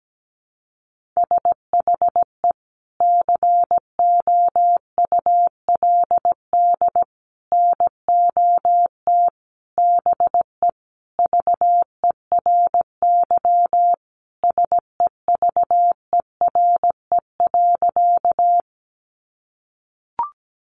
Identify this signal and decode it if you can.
Morse code, heard as a keyed tone, about 17 wpm, 700 Hz. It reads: SHE COULD NOT BE VERY SEVERE.
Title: Sentences from Top 1400 Words 17wpm